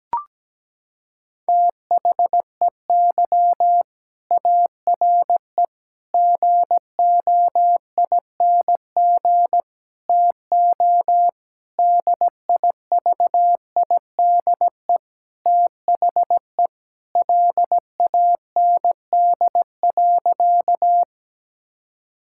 Morse code, heard as a keyed tone, about 17 wpm, 700 Hz. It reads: THEY ARE GOING TO DIVIDE THE LAND.
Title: Sentences from Top 1600 Words 17wpm